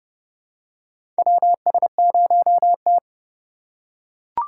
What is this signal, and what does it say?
Morse code, heard as a keyed tone, about 31 wpm, 700 Hz. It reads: WS0T
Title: USA Call Signs - 4 Characters - 2x Character Spacing - 30wpm